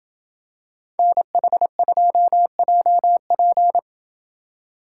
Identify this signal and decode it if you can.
Morse code, heard as a keyed tone, about 28 wpm, 700 Hz. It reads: NH2JP